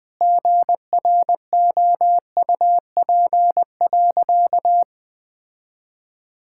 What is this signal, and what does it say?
Morse code, heard as a keyed tone, about 20 wpm, 700 Hz. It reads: GROUP.
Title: Sentences with 5-Word Phrases - Spoken Beginning - 20wpm